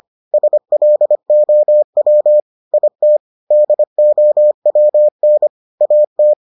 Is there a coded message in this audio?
Yes: Morse code from 0.3 to 6.3 s, heard as a keyed tone, about 25 wpm, 600 Hz. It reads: SLOW IT DOWN AT